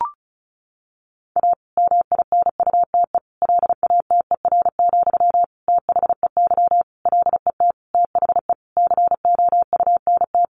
Morse code, heard as a keyed tone, about 34 wpm, 700 Hz. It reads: A MINUTE LATER, THEY LET THE COUNT
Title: Sentences from Top 1500 Words 35wpm